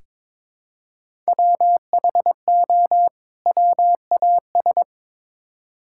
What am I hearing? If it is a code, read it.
Morse code, heard as a keyed tone, about 23 wpm, 700 Hz. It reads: WHO WAS
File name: Build A Phrase - 2-Word Phrases - 22wpm